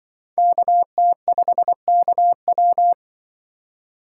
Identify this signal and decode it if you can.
Morse code, heard as a keyed tone, about 24 wpm, 700 Hz. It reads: KT5KW
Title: Speed Racing - USA Call Signs 20wpm